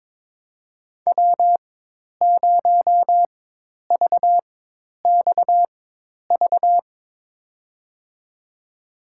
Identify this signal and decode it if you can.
Morse code, heard as a keyed tone, about 22 wpm, spaced as for 10 wpm, 700 Hz. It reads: W0VXV
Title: USA Call Signs - 5 Characters - 4x Character Spacing - 22wpm